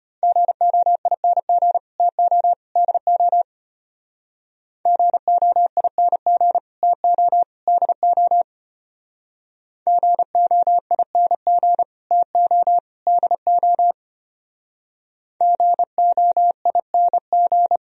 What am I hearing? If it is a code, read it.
Morse code, heard as a keyed tone, about 32 wpm, 700 Hz. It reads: GOING TO DO GOING TO DO GOING TO DO GOING